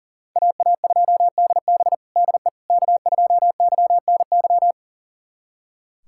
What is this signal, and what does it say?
Morse code, heard as a keyed tone, about 40 wpm, 700 Hz. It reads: AA2DB DE K2YNY
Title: USA Callsign DE Callsign - 5 Characters - 40wpm